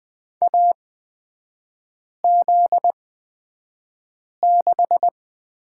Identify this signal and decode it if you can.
Morse code, heard as a keyed tone, about 20 wpm, spaced as for 5 wpm, 700 Hz. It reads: AZ6
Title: 243 Lesson 39e - Letter z - Three Characters 20wpm